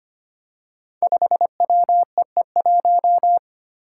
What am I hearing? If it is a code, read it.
Morse code, heard as a keyed tone, about 25 wpm, 700 Hz. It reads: 5WEE1
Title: Five Challenging Letter-Number Combinations 25wpm